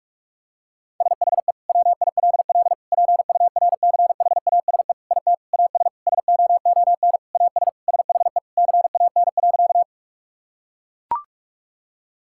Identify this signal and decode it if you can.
Morse code, heard as a keyed tone, about 41 wpm, 690 Hz. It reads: SHE WILL PURCHASE IT AS SOON AS SHE CAN.
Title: Sentences from Top 1000 Words - Review Part 5 - 45wpm